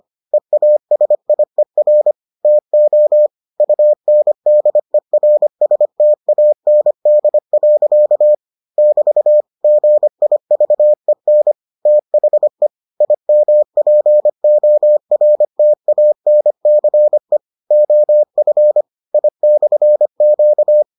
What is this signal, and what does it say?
Morse code, heard as a keyed tone, about 25 wpm, 600 Hz. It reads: EASIER TO UNDERSTAND. <BT> GIVEN THE IMPORTANCE OF I/Q